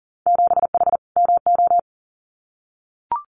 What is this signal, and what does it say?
Morse code, heard as a keyed tone, about 41 wpm, 700 Hz. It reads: 7H MO
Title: Winter Field Day Exchanges Encore 40wpm